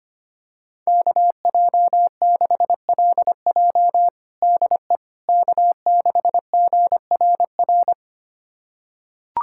Morse code, heard as a keyed tone, about 25 wpm, 700 Hz. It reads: KJ6LJ DE K6GRR
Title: USA Callsign DE Callsign - 5 Characters - 25wpm